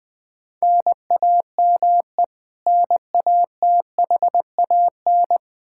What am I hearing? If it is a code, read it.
Morse code, heard as a keyed tone, about 20 wpm, 700 Hz. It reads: NAME NATHAN
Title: Practice QSO - Exchange 1 - Beginning - 20wpm